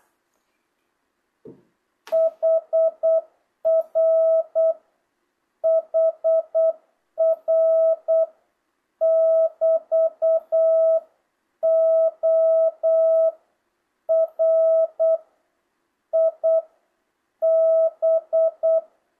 Morse code, heard as a keyed tone, about 8 wpm, 660 Hz. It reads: HR HR <BT>O R I B
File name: jh8cbh / 2025cw